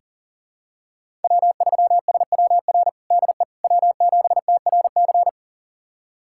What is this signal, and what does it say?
Morse code, heard as a keyed tone, about 40 wpm, 700 Hz. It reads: W3SWR DE W7TRC
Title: USA Callsign DE Callsign - 5 Characters - 40wpm